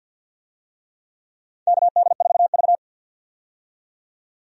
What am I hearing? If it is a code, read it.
Morse code, heard as a keyed tone, about 49 wpm, 700 Hz. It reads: KD4V
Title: USA Call Signs - 4 Characters - 50wpm